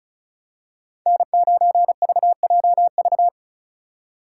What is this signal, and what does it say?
Morse code, heard as a keyed tone, about 34 wpm, 700 Hz. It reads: N9VJV
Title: USA Call Signs - 5 Characters - 3x Character Spacing - 35wpm